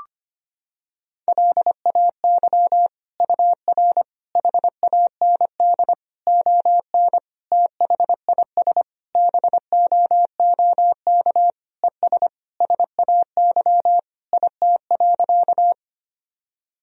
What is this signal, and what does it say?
Morse code, heard as a keyed tone, about 25 wpm, 700 Hz. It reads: LAY UR HAND ON THIS BOOK ES SAY IT.
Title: Sentences from Top 400 Words and Abbreviations 25wpm